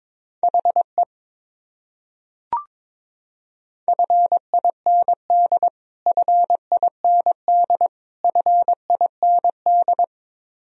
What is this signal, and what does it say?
Morse code, heard as a keyed tone, about 22 wpm, 700 Hz. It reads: HE FIND FIND FIND
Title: Top 100 Words Repeated 22wpm